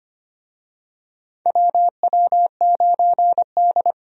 Morse code, heard as a keyed tone, about 25 wpm, 700 Hz. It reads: WW9D
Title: USA Call Signs - 4 Characters - 25wpm